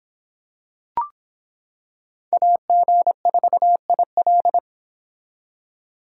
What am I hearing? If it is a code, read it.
Morse code, heard as a keyed tone, about 25 wpm, 700 Hz. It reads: AG4IL